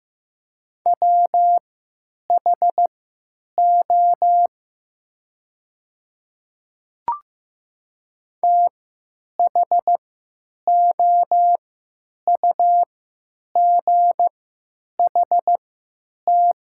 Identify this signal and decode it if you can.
Morse code, heard as a keyed tone, about 15 wpm, spaced as for 9 wpm, 700 Hz. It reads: WHO THOUGHT
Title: 3x Character Spacing - Top 500 Words - 15wpm